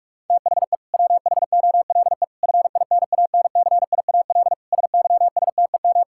Morse code, heard as a keyed tone, about 41 wpm, 700 Hz. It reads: THE WHOLE FINANCIAL SYSTEM
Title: Sentences from Top 1000 Words - Review Part 5 - 45wpm